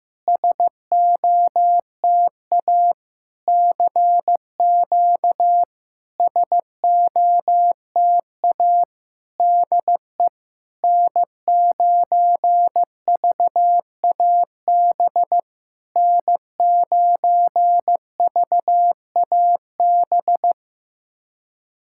Morse code, heard as a keyed tone, about 15 wpm, 700 Hz. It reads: SOTA CQ SOTA DE N9VAB N9VAB
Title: SOTA QSO Practice 15wpm